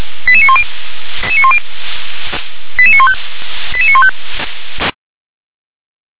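Five-tone selective call (ZVEI), heard as quick runs of five tones, tones of about 70 ms. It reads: 88910 88915 88915
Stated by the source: ZVEI five-tone selective call received on an SDR.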